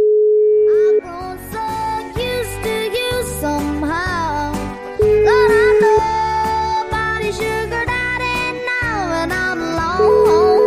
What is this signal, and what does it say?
Ringback tone (Europe, 425 Hz), with noise.